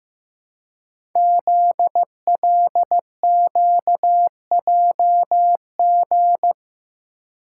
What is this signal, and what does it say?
Morse code, heard as a keyed tone, about 15 wpm, 700 Hz. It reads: ZLQJG
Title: Code Groups - 5 Letters - 15wpm